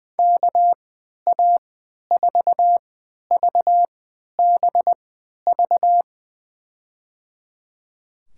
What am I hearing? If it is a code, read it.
Morse code, heard as a keyed tone, about 20 wpm, spaced as for 11 wpm, 700 Hz. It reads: KA4VBV
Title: USA Call Signs - 3x Character Spacing - 20wpm